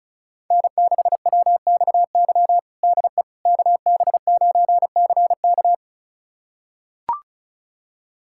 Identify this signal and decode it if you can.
Morse code, heard as a keyed tone, about 36 wpm, 700 Hz. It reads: N6WXY DE KB9CK
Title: USA Callsign DE Callsign - 5 Characters - 35wpm